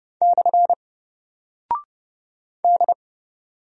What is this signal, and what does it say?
Morse code, heard as a keyed tone, about 30 wpm, 700 Hz. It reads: / D
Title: Single Character (Letter:Number:Punctuation:Prosign) 30wpm